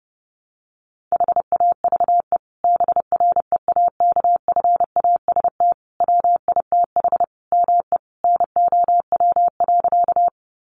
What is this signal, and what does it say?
Morse code, heard as a keyed tone, about 30 wpm, 700 Hz. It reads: HAVE BREAKFAST WITH ME NOW.
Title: Sentences from Top 1500 Words 30wpm